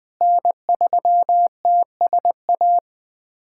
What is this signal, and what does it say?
Morse code, heard as a keyed tone, about 20 wpm, 700 Hz. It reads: N3TSA
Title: USA Callsign DE Callsign - 5 Characters - 20wpm